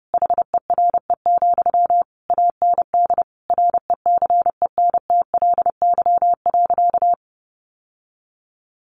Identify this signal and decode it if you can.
Morse code, heard as a keyed tone, about 28 wpm, 700 Hz. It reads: HERE, AND RECENTLY.
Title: Sentences from Top 1200 Words 30wpm